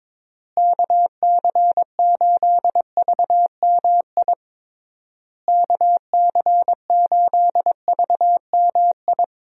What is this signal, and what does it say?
Morse code, heard as a keyed tone, about 22 wpm, 700 Hz. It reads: KC8VMI KC8VMI
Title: USA Call Signs - 6 Characters - Repeated 3x - 22wpm